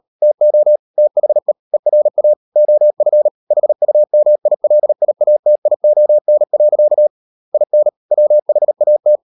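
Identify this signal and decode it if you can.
Morse code, heard as a keyed tone, about 40 wpm, 600 Hz. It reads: TO THE ERA OF HUMILIATION. IN WHAT
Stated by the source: Morse code practice at 35–39 wpm sending English text of Ken Follett's novel Never, from CW med Gnister.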